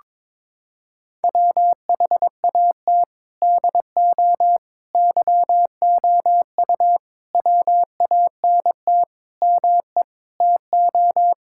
Morse code, heard as a keyed tone, about 22 wpm, 700 Hz. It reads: WHAT DO YOU WANT ME TO
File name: practice / 22wpm / Sets of 7 Words - 22wpm